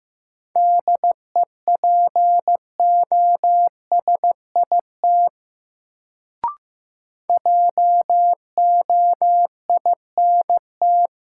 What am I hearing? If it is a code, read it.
Morse code, heard as a keyed tone, about 15 wpm, 700 Hz. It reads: DEPOSIT JOINT
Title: Sentences from Top 1300 Words - New Word Drill Alt - 15wpm